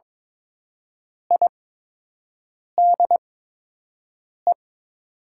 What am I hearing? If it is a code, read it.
Morse code, heard as a keyed tone, about 22 wpm, spaced as for 6 wpm, 700 Hz. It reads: IDE
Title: 8x Character Spacing - Top 100 Words - 22wpm